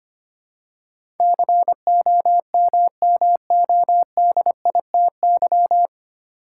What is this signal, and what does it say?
Morse code, heard as a keyed tone, about 25 wpm, 700 Hz. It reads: COMMODITY
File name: Nine Letter Words - 25wpm